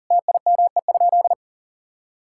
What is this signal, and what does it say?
Morse code, heard as a keyed tone, about 37 wpm, 700 Hz. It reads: TIME?
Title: Sentences from Top 1000 Words - Review Part 5 - 40wpm